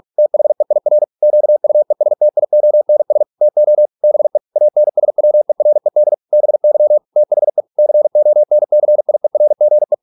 Morse code, heard as a keyed tone, about 46 wpm, 600 Hz. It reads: THEIR QUESTIONS TO BE ANSWERED BY THE CONCIERGE